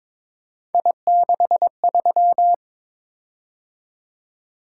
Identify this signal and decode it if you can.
Morse code, heard as a keyed tone, about 22 wpm, 700 Hz. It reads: I63
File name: practice / 22wpm / Three Letter-Number Combinations - 22wpm